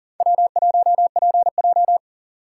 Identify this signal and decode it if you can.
Morse code, heard as a keyed tone, about 40 wpm, 700 Hz. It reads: W1PJ
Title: USA Call Signs - 4 Characters - 4x Character Spacing - 40wpm